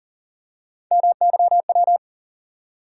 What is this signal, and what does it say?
Morse code, heard as a keyed tone, about 40 wpm, 700 Hz. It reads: MYW
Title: USA Call Sign Postfixes - 3 Letters 40wpm